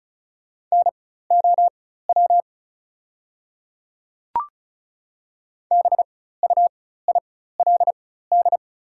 Morse code, heard as a keyed tone, about 34 wpm, spaced as for 16 wpm, 700 Hz. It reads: NOW BUILD
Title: 4x Character Spacing - Top 200 Words - 35wpm